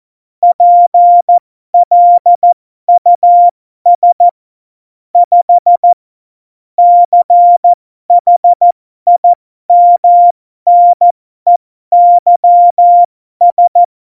Morse code, heard as a keyed tone, about 14 wpm, spaced as for 12 wpm, 700 Hz. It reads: PLUS 5 CHIMNEYS
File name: Morsekurs / Witze